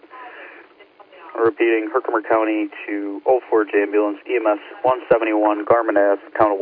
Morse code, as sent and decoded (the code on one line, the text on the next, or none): none